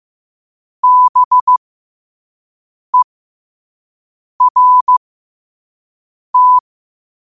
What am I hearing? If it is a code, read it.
Morse code, heard as a keyed tone, about 15 wpm, spaced as for 5 wpm, 1000 Hz. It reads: BERT